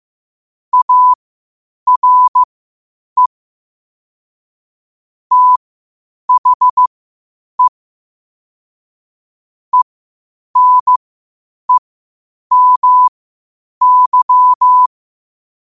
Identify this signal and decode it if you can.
Morse code, heard as a keyed tone, about 15 wpm, spaced as for 8 wpm, 1000 Hz. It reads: ARE THE ENEMY
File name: qotdfiles / 8wpm